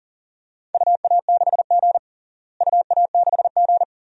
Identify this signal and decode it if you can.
Morse code, heard as a keyed tone, about 40 wpm, 700 Hz. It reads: UA6G UA6G
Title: DXCC-100 Call Signs - 4 Characters - Repeated 2x - 40wpm